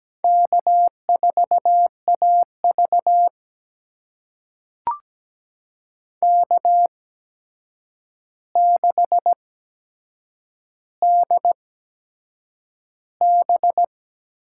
Morse code, heard as a keyed tone, about 17 wpm, spaced as for 7 wpm, 700 Hz. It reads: K4AV K6DB
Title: USA Call Signs - 4 Characters - 8x Character Spacing - 17wpm